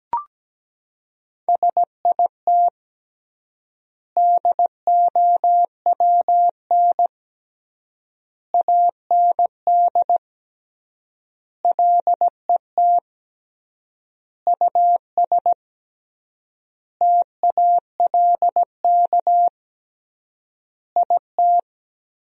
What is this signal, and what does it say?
Morse code, heard as a keyed tone, about 17 wpm, 700 Hz. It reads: SIT DOWN AND LET US TALK IT
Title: 3x Word-Spacing - Sentences from Top 500 Words - 17wpm